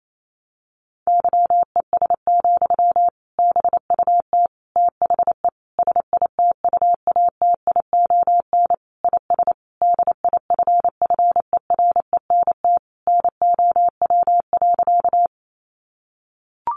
Morse code, heard as a keyed tone, about 28 wpm, 700 Hz. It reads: YES, BUT THE SITUATION IS DIFFERENT NOW.
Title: Sentences from Top 1200 Words 28wpm